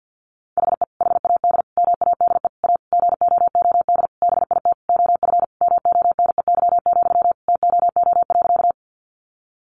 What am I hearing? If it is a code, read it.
Morse code, heard as a keyed tone, about 50 wpm, 700 Hz. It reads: HE HAD MADE A GOOD BIT OF MONEY, TOO.
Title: Sentences from Top 900 Words 50wpm